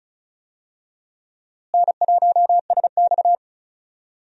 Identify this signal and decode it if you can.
Morse code, heard as a keyed tone, about 34 wpm, 700 Hz. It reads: N1SX
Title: USA Call Signs - 4 Characters - Repeated 3x - 35wpm